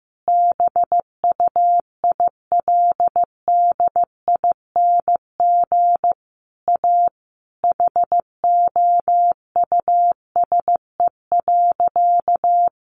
Morse code, heard as a keyed tone, about 15 wpm, 700 Hz. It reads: BUILDING A HOUSE.
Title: Sentences from Top 700 Words 15wpm